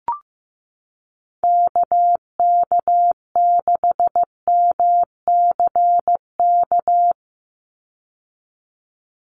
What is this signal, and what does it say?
Morse code, heard as a keyed tone, about 15 wpm, 700 Hz. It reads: KK6MCK
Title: USA Call Signs 15wpm